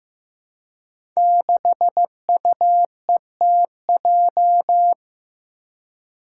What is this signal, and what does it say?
Morse code, heard as a keyed tone, about 15 wpm, 700 Hz. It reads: 6UETJ
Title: Five Challenging Letter-Number Combinations 15wpm